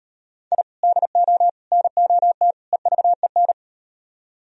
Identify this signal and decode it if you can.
Morse code, heard as a keyed tone, about 36 wpm, 700 Hz. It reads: I DO NOT EVEN